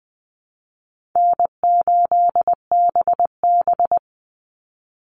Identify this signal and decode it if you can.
Morse code, heard as a keyed tone, about 20 wpm, 700 Hz. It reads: N8BB
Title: NA SOTA Chaser Callsigns 20wpm